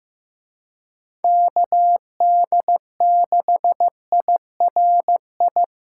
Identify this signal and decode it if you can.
Morse code, heard as a keyed tone, about 15 wpm, 700 Hz. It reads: KD6IRI
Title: K1USN SST QSO Practice - 15wpm